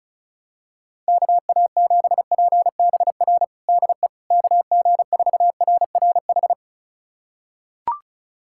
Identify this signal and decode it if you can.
Morse code, heard as a keyed tone, about 34 wpm, 700 Hz. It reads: KA7PBR DE KG4RRH
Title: USA Callsign DE Callsign - 6 Characters - 35wpm